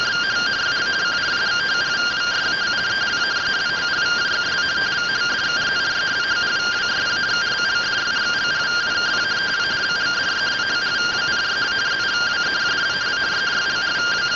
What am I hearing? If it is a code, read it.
Two-tone teleprinter signal, about 170 Hz shift.